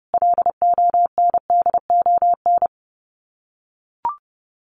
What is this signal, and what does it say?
Morse code, heard as a keyed tone, about 30 wpm, 700 Hz. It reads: LONDON
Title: Common QSO Phrases 30wpm